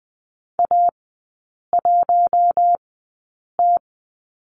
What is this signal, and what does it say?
Morse code, heard as a keyed tone, about 20 wpm, spaced as for 8 wpm, 700 Hz. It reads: A1T
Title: 047 Lesson 9e - Number 4 - Three Characters 20wpm